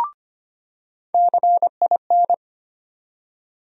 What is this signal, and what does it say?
Morse code, heard as a keyed tone, about 25 wpm, 700 Hz. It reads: CIN
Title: USA Call Sign Postfixes - 3 Letters 25wpm